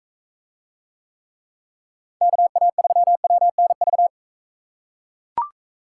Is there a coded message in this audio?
Yes: Morse code from 2.2 to 4.1 s, heard as a keyed tone, about 42 wpm, 700 Hz. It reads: KA3WNV